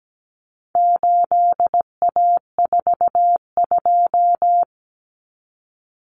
Morse code, heard as a keyed tone, about 17 wpm, 700 Hz. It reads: 8A42
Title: Common Cut Numbers - 4-Digit 17wpm